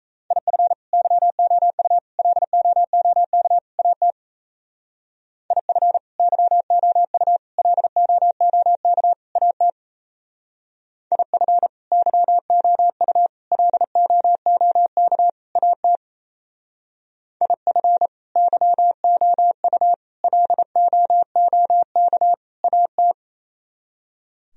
Morse code, heard as a keyed tone, about 34 wpm, 700 Hz. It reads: IF YOU LOOK AT IF YOU LOOK AT IF YOU LOOK AT IF YOU LOOK AT
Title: Speed-Racing - Sets of 4 Words 28wpm